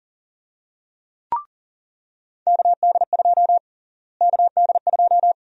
Morse code, heard as a keyed tone, about 40 wpm, 700 Hz. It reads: KD2 KD2
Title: USA Call Sign Prefixes - Repeated 2x - 40wpm